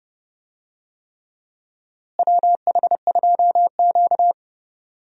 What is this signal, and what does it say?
Morse code, heard as a keyed tone, about 32 wpm, 700 Hz. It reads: WH2Q
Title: USA Call Signs - 4 Characters - Repeated 3x - 30wpm